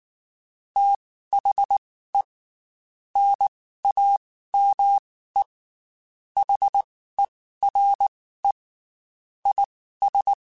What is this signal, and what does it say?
Morse code, heard as a keyed tone, about 19 wpm, spaced as for 14 wpm, 800 Hz. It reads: THE NAME HERE IS
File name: QSOs / QSOs-mp3s / WD0BGP 1@18